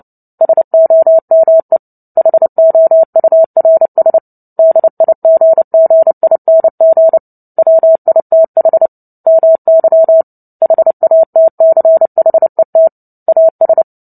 Morse code, heard as a keyed tone, about 30 wpm, 650 Hz. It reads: SOME HOURS DIGGING WITH MY HATCHET AS
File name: Book Two / 03-Chapter 3 - 29 WPM